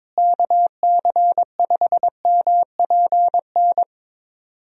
Morse code, heard as a keyed tone, about 22 wpm, 700 Hz. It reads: KC5MPN
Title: USA Call Signs - 6 Characters - Repeated 2x - 22wpm